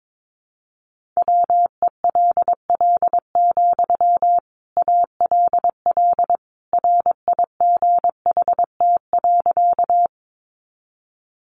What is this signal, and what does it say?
Morse code, heard as a keyed tone, about 22 wpm, 700 Hz. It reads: WELL, ALL RIGHT.
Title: Sentences from Top 200 Words 22wpm